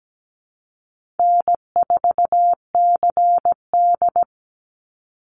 Morse code, heard as a keyed tone, about 17 wpm, 700 Hz. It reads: N4CD
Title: Easy Call Signs - 17wpm